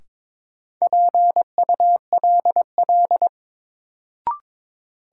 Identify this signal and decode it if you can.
Morse code, heard as a keyed tone, about 23 wpm, 700 Hz. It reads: PULL
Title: Sentences from Top 400 Words - New Word Drill - 22wpm